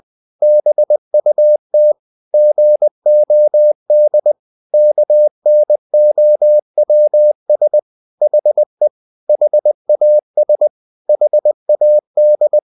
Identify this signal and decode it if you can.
Morse code, heard as a keyed tone, about 20 wpm, 600 Hz. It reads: BUT GOD KNOWS HE HAS HAD
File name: the old man and the sea part 2 20wpm